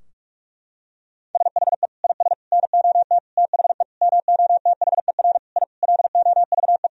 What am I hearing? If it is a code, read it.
Morse code, heard as a keyed tone, about 46 wpm, 700 Hz. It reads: SHE IS NOT THE MOTHER I LOVE
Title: Sentences from Top 1000 Words - Review Part 2 - 45wpm